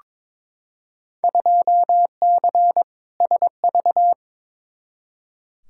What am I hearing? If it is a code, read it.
Morse code, heard as a keyed tone, about 22 wpm, 700 Hz. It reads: 2C SV
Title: Field Day Exchanges Encore - 22wpm